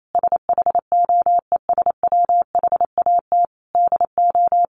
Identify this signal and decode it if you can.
Morse code, heard as a keyed tone, about 28 wpm, 700 Hz. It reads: SHOES WHAT DO
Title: A Sound Of Thunder 28wpm